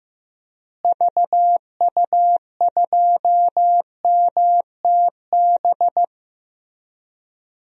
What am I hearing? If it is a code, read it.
Morse code, heard as a keyed tone, about 15 wpm, 700 Hz. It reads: VU2MTB
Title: DXCC-100 Call Signs - 6 Characters - Repeated 3x - 15wpm